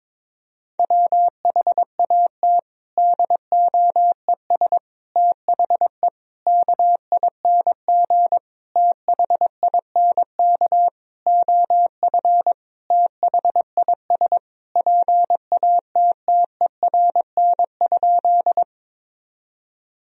Morse code, heard as a keyed tone, about 22 wpm, 700 Hz. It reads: WHAT DOES THE KING THINK OF THIS PATTERN?
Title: Sentences from Top 1000 Words - Review Part 2 - 22wpm